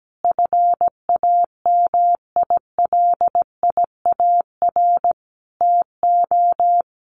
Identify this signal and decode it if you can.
Morse code, heard as a keyed tone, about 17 wpm, 700 Hz. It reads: FAMILIAR TO
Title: Sentences from Top 1100 Words 17wpm